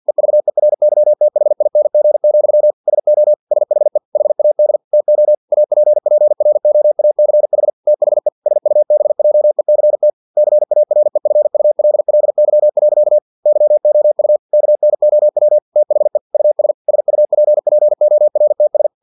Morse code, heard as a keyed tone, about 50 wpm, 600 Hz. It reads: EVERYTHING, SO SHE HAD TO APPROACH THE SUBJECT CAREFULLY. YOU KNOW THE US SUPPORTS
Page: Never 49wpm